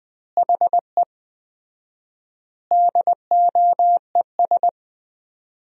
Morse code, heard as a keyed tone, about 20 wpm, 700 Hz. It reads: HE DOES
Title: 4x Word-Spacing - Sentences from Top 200 Words - 20wpm